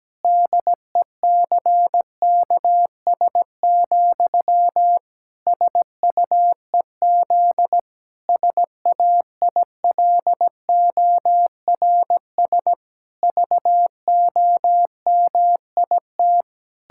Morse code, heard as a keyed tone, about 17 wpm, 700 Hz. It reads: DECKS, SUEZ SAILORS VOMIT
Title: Pangram Challenge v2 - 17wpm